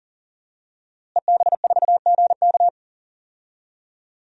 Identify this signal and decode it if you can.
Morse code, heard as a keyed tone, about 40 wpm, 700 Hz. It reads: EB4GK
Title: DXCC-100 Call Signs - 5 Characters - Repeated 2x - 40wpm